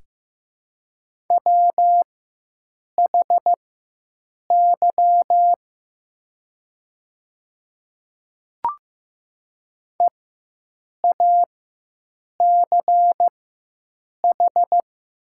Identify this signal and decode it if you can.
Morse code, heard as a keyed tone, about 15 wpm, spaced as for 7 wpm, 700 Hz. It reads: WHY EACH
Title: 4x Character Spacing - Top 300 Words - 15wpm